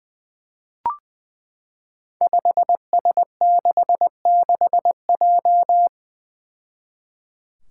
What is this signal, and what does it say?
Morse code, heard as a keyed tone, about 20 wpm, 700 Hz. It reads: 5S66J
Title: Five Challenging Letter-Number Combinations 20wpm